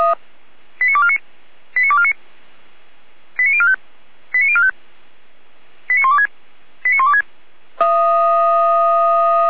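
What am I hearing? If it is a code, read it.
Five-tone selective call (ZVEI), heard as quick runs of five tones, tones of about 70 ms. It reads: 89249 89249 89045 89045 89126 89126
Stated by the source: ZVEI five-tone selective call, code 89115, received on an SDR.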